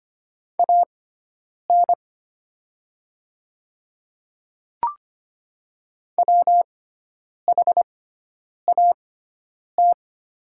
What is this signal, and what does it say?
Morse code, heard as a keyed tone, about 25 wpm, spaced as for 9 wpm, 700 Hz. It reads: AN WHAT
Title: 6x Character Spacing - Top 100 Words - 25wpm